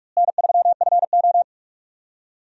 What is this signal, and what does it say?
Morse code, heard as a keyed tone, about 44 wpm, 700 Hz. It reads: N3FO
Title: USA Call Signs - 4 Characters - Repeated 2x - 45wpm